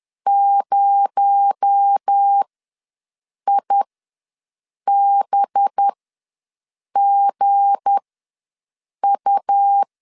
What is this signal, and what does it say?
Morse code, heard as a keyed tone, about 10 wpm, spaced as for 6 wpm, 800 Hz. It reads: 0IBGU
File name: LEZIONI / 7 GF9Z / 7 GF9 ws 10wpm 4034